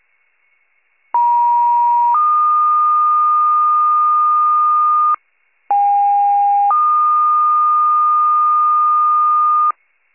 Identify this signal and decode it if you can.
Two-tone page, heard as a one-second tone, then a three-second tone, about 950 then 1220 Hz, about 810 then 1220 Hz. It is heard from 1.1 to 9.7 s.